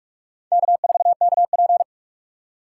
Morse code, heard as a keyed tone, about 43 wpm, 700 Hz. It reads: K4KP
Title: USA Call Signs - 4 Characters - 6x Character Spacing - 45wpm